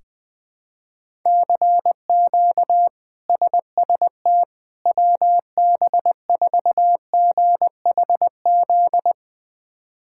Morse code, heard as a keyed tone, about 20 wpm, 700 Hz. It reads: CQ SST WB4GHZ
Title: K1USN SST QSO Practice - 20wpm